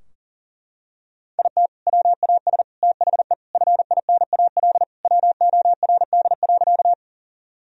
Morse code, heard as a keyed tone, about 40 wpm, 700 Hz. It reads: IT WAS THE FINAL WORD.
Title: Sentences from Top 1000 Words - Review Part 2 - 40wpm